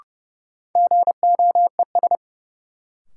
Morse code, heard as a keyed tone, about 30 wpm, 700 Hz. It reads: GOES
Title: Sentences from Top 1000 Words - Review Part 5 - New Word Drill - 30wpm